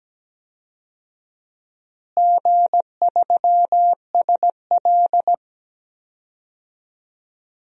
Morse code, heard as a keyed tone, about 17 wpm, 700 Hz. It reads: G3SL
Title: DXCC-100 Call Signs - 4 Characters - Repeated 2x - 17wpm